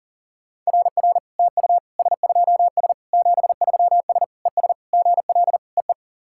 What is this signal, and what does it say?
Morse code, heard as a keyed tone, about 37 wpm, 700 Hz. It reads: RR TU S2S 73S ES GL EE
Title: SOTA QSO Practice 40wpm